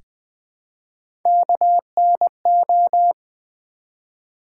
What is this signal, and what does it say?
Morse code, heard as a keyed tone, about 20 wpm, 700 Hz. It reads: KNO
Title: Build A Word - 1 Letter At A Time - 5-Letter Words - 20wpm